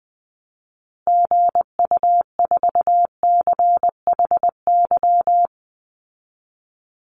Morse code, heard as a keyed tone, about 20 wpm, 700 Hz. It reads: GU4CHY
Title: Call Signs Repeated 3x - 20wpm